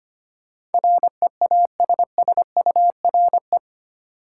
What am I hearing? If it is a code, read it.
Morse code, heard as a keyed tone, about 25 wpm, 700 Hz. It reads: REASSURE
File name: Eight Letter Words - 25wpm